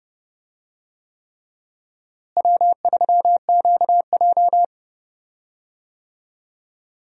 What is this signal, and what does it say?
Morse code, heard as a keyed tone, about 30 wpm, 700 Hz. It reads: W3QJ
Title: USA Call Signs - 4 Characters - Repeated 3x - 30wpm